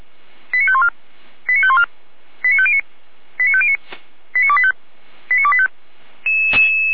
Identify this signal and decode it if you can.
Five-tone selective call (ZVEI), heard as quick runs of five tones, tones of about 70 ms. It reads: 89514 89514 89509 89509 89285 89285